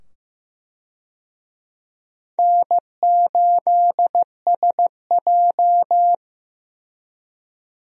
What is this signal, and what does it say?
Morse code, heard as a keyed tone, about 15 wpm, 700 Hz. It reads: N8SJ
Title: USA Call Signs - 4 Characters - 15wpm